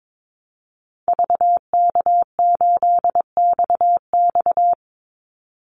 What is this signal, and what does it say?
Morse code, heard as a keyed tone, about 22 wpm, 700 Hz. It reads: VK8XX
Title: Call Signs Repeated 2x - 22wpm